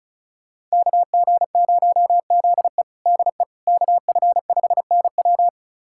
Morse code, heard as a keyed tone, about 34 wpm, 700 Hz. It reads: KG0ZE DE KF5NW
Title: USA Callsign DE Callsign - 5 Characters - 35wpm